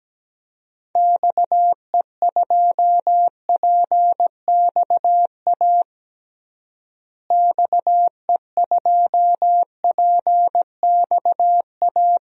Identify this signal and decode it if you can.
Morse code, heard as a keyed tone, about 17 wpm, 700 Hz. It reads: XE2PXA XE2PXA
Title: DXCC-100 Call Signs - 6 Characters - Repeated 3x - 17wpm